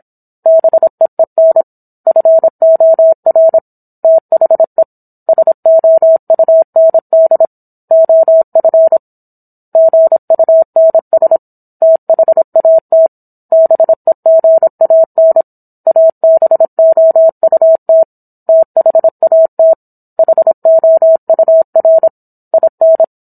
Morse code, heard as a keyed tone, about 27 wpm, 650 Hz. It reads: BEEN FOR THE SOUND OF GUNS THAT BEGAN ABOUT THAT HOUR IN